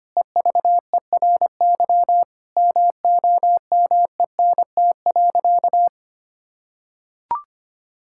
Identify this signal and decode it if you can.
Morse code, heard as a keyed tone, about 25 wpm, 700 Hz. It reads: EVERY MOMENT.
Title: Sentences from Top 1000 Words - Review Part 5 - 25wpm